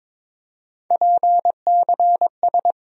Morse code, heard as a keyed tone, about 22 wpm, 700 Hz. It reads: PCS
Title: USA Call Sign Postfixes - 3 Letters 22wpm